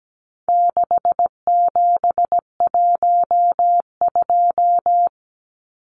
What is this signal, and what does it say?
Morse code, heard as a keyed tone, about 17 wpm, 700 Hz. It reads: 6712
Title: Numbers - 4-Digit 17wpm